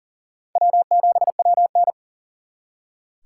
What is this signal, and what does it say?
Morse code, heard as a keyed tone, about 52 wpm, 700 Hz. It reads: W7WN